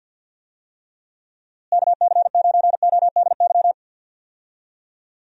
Morse code, heard as a keyed tone, about 47 wpm, 700 Hz. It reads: KC9ODY
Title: USA Call Signs - 6 Characters - Repeated 3x - 50wpm